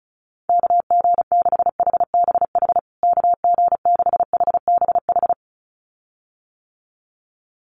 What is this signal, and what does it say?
Morse code, heard as a keyed tone, about 34 wpm, 700 Hz. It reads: KG6HBH KG6HBH
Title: USA Call Signs Repeated 2x 35wpm